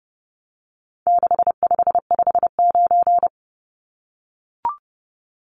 Morse code, heard as a keyed tone, about 30 wpm, 700 Hz. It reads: 6559
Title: Numbers - 4-Digit 30wpm